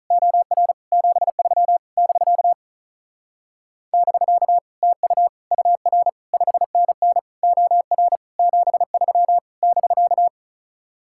Morse code, heard as a keyed tone, about 45 wpm, 700 Hz. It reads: OR 73 <BK> <BK> TU UR 5NN OR 73 <BK>